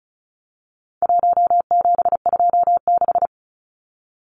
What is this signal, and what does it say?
Morse code, heard as a keyed tone, about 34 wpm, 700 Hz. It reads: 1726
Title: Numbers - 4-Digit 35wpm